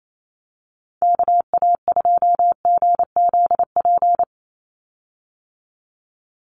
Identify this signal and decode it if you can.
Morse code, heard as a keyed tone, about 28 wpm, 700 Hz. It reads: KA2GZP
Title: USA Call Signs 28wpm